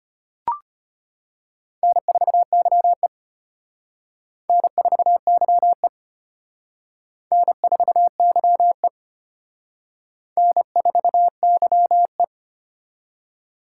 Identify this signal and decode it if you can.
Morse code, heard as a keyed tone, about 32 wpm, 700 Hz. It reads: N4YE N4YE N4YE N4YE